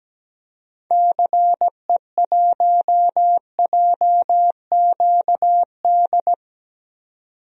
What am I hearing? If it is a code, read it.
Morse code, heard as a keyed tone, about 17 wpm, 700 Hz. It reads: CE1JQD
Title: DXCC-100 Call Signs - 6 Characters - 17wpm